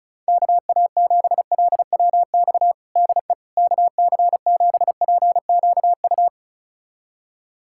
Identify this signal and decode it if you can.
Morse code, heard as a keyed tone, about 34 wpm, 700 Hz. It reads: KA7LWX DE KC7PQU
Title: USA Callsign DE Callsign - 6 Characters - 35wpm